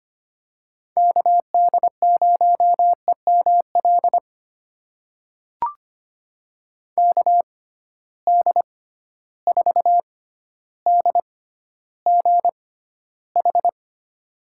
Morse code, heard as a keyed tone, about 25 wpm, spaced as for 13 wpm, 700 Hz. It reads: KD0EML KD4DGH
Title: USA Call Signs - 6 Characters - 6x Character Spacing - 25wpm